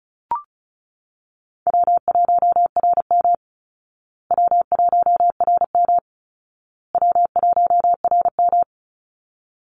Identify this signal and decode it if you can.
Morse code, heard as a keyed tone, about 34 wpm, 700 Hz. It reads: W1RM W1RM W1RM
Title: Easy Call Signs Repeated 3x - 35wpm